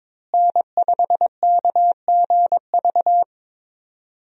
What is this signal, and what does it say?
Morse code, heard as a keyed tone, about 22 wpm, 700 Hz. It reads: N5KGV
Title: USA Call Signs - 5 Characters - Repeated 2x - 22wpm